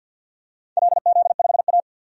Morse code, heard as a keyed tone, about 50 wpm, 700 Hz. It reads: LZ5A
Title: DXCC-100 Call Signs - 4 Characters - 50wpm